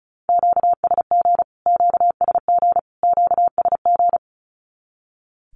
Morse code, heard as a keyed tone, about 34 wpm, 700 Hz. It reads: QSG QSG QSG
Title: Q-Codes Repeated 3x 35wpm